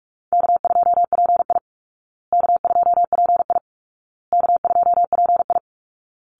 Morse code, heard as a keyed tone, about 44 wpm, 700 Hz. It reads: K2PI K2PI K2PI
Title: Easy Call Signs Repeated 3x - 45wpm